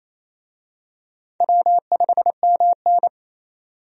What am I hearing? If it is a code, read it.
Morse code, heard as a keyed tone, about 28 wpm, 700 Hz. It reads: W5MN
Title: USA Call Signs - 4 Characters - 28wpm